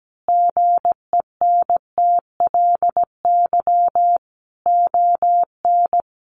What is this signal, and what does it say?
Morse code, heard as a keyed tone, about 17 wpm, 700 Hz. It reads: GENTLY ON
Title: Sentences from Top 1500 Words 17wpm